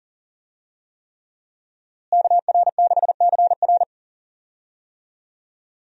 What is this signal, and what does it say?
Morse code, heard as a keyed tone, about 39 wpm, 700 Hz. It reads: KR6CR
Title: USA Call Signs - 5 Characters - Repeated 3x - 40wpm